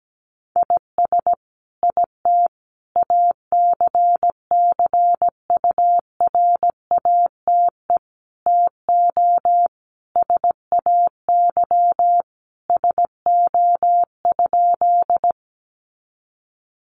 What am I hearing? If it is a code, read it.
Morse code, heard as a keyed tone, about 17 wpm, 700 Hz. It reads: IS IT ACCURATE TO SAY SO?
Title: Sentences from Top 1200 Words 17wpm